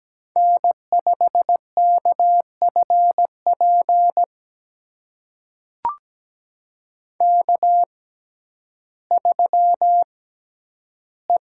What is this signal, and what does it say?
Morse code, heard as a keyed tone, about 17 wpm, spaced as for 10 wpm, 700 Hz. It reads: N5KFP K3E
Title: USA Call Signs - 5 Characters - 6x Character Spacing - 17wpm